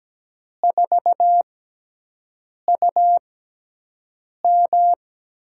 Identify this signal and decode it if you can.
Morse code, heard as a keyed tone, about 17 wpm, spaced as for 6 wpm, 700 Hz. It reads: 4UM
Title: USA Call Signs - 5 Characters - 6x Character Spacing - 17wpm